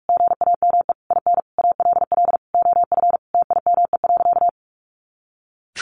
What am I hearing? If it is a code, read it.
Morse code, heard as a keyed tone, about 44 wpm, 700 Hz. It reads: GAME IN ALL OF TIME.
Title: A Sound Of Thunder 45wpm